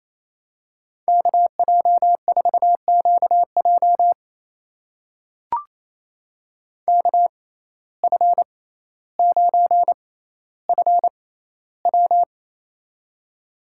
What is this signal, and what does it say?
Morse code, heard as a keyed tone, about 28 wpm, spaced as for 14 wpm, 700 Hz. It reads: KJ4QJ KF9FW
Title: USA Call Signs - 5 Characters - 6x Character Spacing - 28wpm